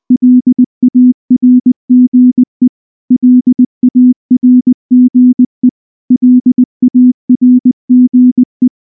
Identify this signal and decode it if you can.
Morse code, heard as a keyed tone, about 19 wpm, 250 Hz. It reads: LARGE LARGE LARGE
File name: morse / morse-training-20WPM-2